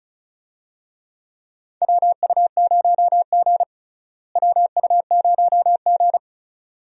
Morse code, heard as a keyed tone, about 36 wpm, 700 Hz. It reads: WU0G WU0G
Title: USA Call Signs - 4 Characters - Repeated 2x - 35wpm